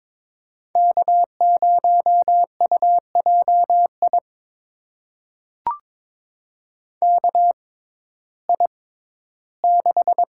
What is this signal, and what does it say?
Morse code, heard as a keyed tone, about 22 wpm, spaced as for 14 wpm, 700 Hz. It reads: K0UJI KI6
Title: USA Call Signs - 6x Character Spacing - 22wpm